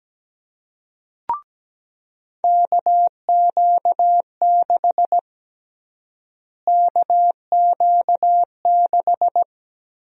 Morse code, heard as a keyed tone, about 17 wpm, 700 Hz. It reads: KQ6 KQ6
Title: USA Call Sign Prefixes - Repeated 2x - 17wpm